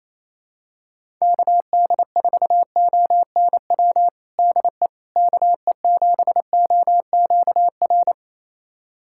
Morse code, heard as a keyed tone, about 28 wpm, 700 Hz. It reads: KD4ONW DE KE7OQR
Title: USA Callsign DE Callsign - 6 Characters - 28wpm